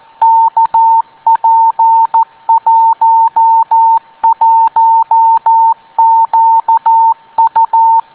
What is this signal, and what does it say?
Morse code, heard as a keyed tone, about 14 wpm, 910 Hz. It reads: KP11QU